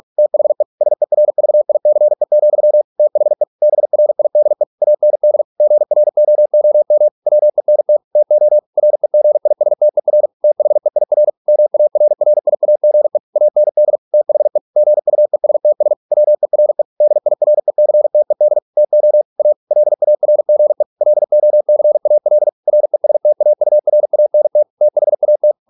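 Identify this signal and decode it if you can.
Morse code, heard as a keyed tone, about 46 wpm, 600 Hz. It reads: THE SERVICE, THE BRIDE AND GROOM WENT TO REGISTER THEIR MARRIAGE AND THE GUESTS WERE DIRECTED TO A LARGE LOCAL RESTAURANT THAT
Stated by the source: Morse practice at 45 wpm, the novel Never by Ken Follett in English, from CW med Gnister.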